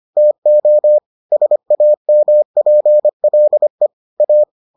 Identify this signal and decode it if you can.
Morse code, heard as a keyed tone, about 25 wpm, 600 Hz. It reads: TO SAMPLE A